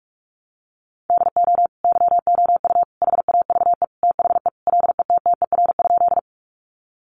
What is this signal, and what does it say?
Morse code, heard as a keyed tone, about 44 wpm, 700 Hz. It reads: DO YOU HAVE THE LETTER?
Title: Sentences from Top 300 Words 45wpm